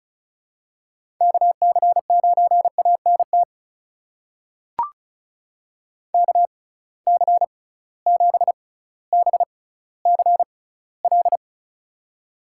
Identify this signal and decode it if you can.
Morse code, heard as a keyed tone, about 36 wpm, spaced as for 18 wpm, 700 Hz. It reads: KC9ANT KC7BCL
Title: USA Call Signs - 6 Characters - 6x Character Spacing - 35wpm